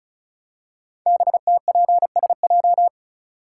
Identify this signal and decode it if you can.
Morse code, heard as a keyed tone, about 36 wpm, 700 Hz. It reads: BTPSJ
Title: Code Groups - 5 Letters - 35wpm